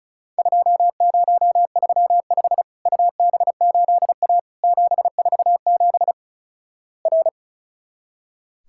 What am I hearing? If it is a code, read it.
Morse code, heard as a keyed tone, about 36 wpm, 700 Hz. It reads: 2035 UB8A 747 R
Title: Worked All Europe DX Contest - QTC Exchange - 35wpm